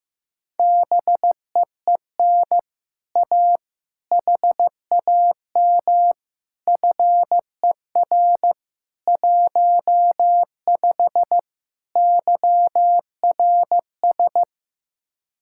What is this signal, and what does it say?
Morse code, heard as a keyed tone, about 15 wpm, 700 Hz. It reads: BEEN A HAM FER 15 YRS
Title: Speed-Racing Common QSO Phrases Encore - 15wpm